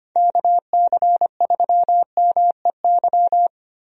Morse code, heard as a keyed tone, about 25 wpm, 700 Hz. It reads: KC3MEY
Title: USA Call Signs - 6 Characters - Repeated 3x - 25wpm